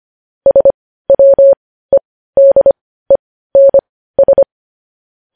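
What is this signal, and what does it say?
Morse code, heard as a keyed tone, about 25 wpm, spaced as for 15 wpm, 550 Hz. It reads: SWEDENS
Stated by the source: Morse code news headlines at 15 WPM.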